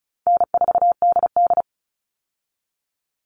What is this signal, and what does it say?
Morse code, heard as a keyed tone, about 34 wpm, 700 Hz. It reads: N4DD
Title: Easy Call Signs Repeated 3x - 35wpm